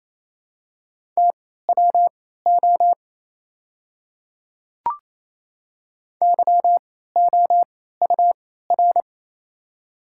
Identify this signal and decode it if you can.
Morse code, heard as a keyed tone, about 28 wpm, spaced as for 16 wpm, 700 Hz. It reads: TWO YOUR
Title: 3x Character Spacing - Top 200 Words - 28wpm